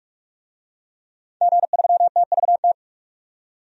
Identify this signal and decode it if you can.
Morse code, heard as a keyed tone, about 46 wpm, 700 Hz. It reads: G3TVT
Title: DXCC-100 Call Signs - 5 Characters - 45wpm